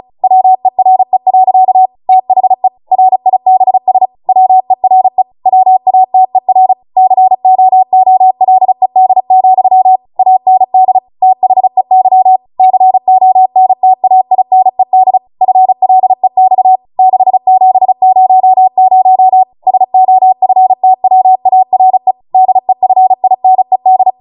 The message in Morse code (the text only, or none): WERE. THE RIBS WERE WATER COOLED, AND THEY CONTAINED FLEX 6700 SOFTWARE DEFINED